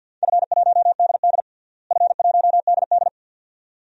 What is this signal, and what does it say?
Morse code, heard as a keyed tone, about 53 wpm, 700 Hz. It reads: F1DD F1DD
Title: DXCC-100 Call Signs - 4 Characters - Repeated 2x - 50wpm